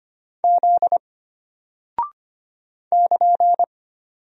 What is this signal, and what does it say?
Morse code, heard as a keyed tone, about 25 wpm, 700 Hz. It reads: Z <KN>
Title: Single Character (Letter:Number:Punctuation:Prosign) Alt 25wpm